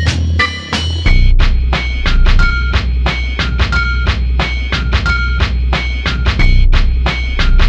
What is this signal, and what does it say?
There is no code.